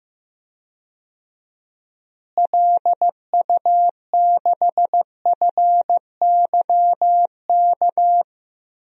Morse code, heard as a keyed tone, about 15 wpm, 700 Hz. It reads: LU6FYK
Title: DXCC-100 Call Signs - 6 Characters - Repeated 3x - 15wpm